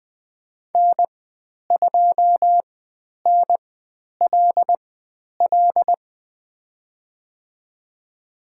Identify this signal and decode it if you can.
Morse code, heard as a keyed tone, about 20 wpm, spaced as for 10 wpm, 700 Hz. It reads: N2NLL
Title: 109 Lesson 18g - Letter m - Call Signs 20wpm